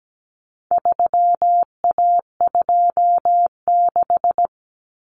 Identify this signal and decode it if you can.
Morse code, heard as a keyed tone, about 17 wpm, 700 Hz. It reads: 3A26
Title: Common Cut Numbers - 4-Digit 17wpm